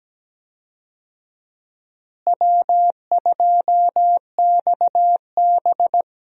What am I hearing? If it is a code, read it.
Morse code, heard as a keyed tone, about 17 wpm, 700 Hz. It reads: W2XB
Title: USA Call Signs - 4 Characters - Repeated 3x - 17wpm